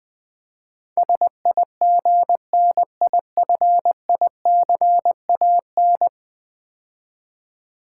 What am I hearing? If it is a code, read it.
Morse code, heard as a keyed tone, about 20 wpm, 700 Hz. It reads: SIGNIFICAN
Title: Build A Word - 5 Letters At A Time - 12-Letter Words - 20wpm